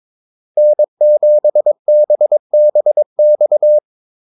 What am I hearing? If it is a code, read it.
Morse code, heard as a keyed tone, about 22 wpm, 600 Hz. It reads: N7BBX
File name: Callsigns-22wpm